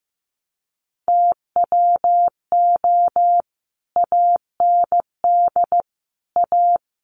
Morse code, heard as a keyed tone, about 15 wpm, 700 Hz. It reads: TWO AND A
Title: Sets of 5 Words 15wpm